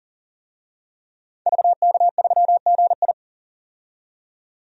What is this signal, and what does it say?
Morse code, heard as a keyed tone, about 40 wpm, 700 Hz. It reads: VK3GI
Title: DXCC-100 Call Signs - 5 Characters - 40wpm